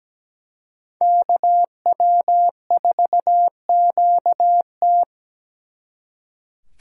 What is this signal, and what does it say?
Morse code, heard as a keyed tone, about 20 wpm, 700 Hz. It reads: KW4QT
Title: Speed Racing - USA Call Signs 17wpm